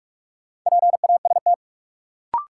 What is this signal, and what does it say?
Morse code, heard as a keyed tone, about 46 wpm, 700 Hz. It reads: PAST
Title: Sets of 2 Words - New Word Drill - 45wpm